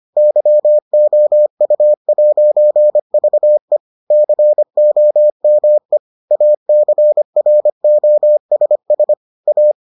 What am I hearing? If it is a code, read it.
Morse code, heard as a keyed tone, about 25 wpm, 600 Hz. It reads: YOU'VE COME ACROSS A